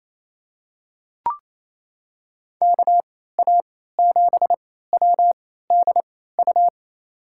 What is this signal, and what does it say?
Morse code, heard as a keyed tone, about 28 wpm, spaced as for 16 wpm, 700 Hz. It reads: KA7WDU
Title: USA Call Signs - 6 Characters - 3x Character Spacing - 28wpm